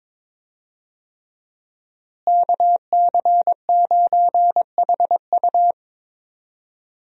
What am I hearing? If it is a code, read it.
Morse code, heard as a keyed tone, about 22 wpm, 700 Hz. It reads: KC9HU
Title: USA Call Signs - 5 Characters - Repeated 3x - 22wpm